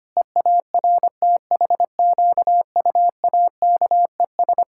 Morse code, heard as a keyed tone, about 25 wpm, 700 Hz. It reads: EARTHQUAKES
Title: Pangram Challenge v2 - 25wpm